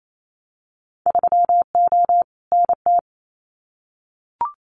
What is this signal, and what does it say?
Morse code, heard as a keyed tone, about 28 wpm, 700 Hz. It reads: 3O NT
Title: Winter Field Day Exchanges Encore 28wpm